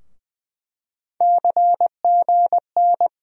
Morse code, heard as a keyed tone, about 21 wpm, 700 Hz. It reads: CGN
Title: USA Call Sign Postfixes - 3 Letters 20wpm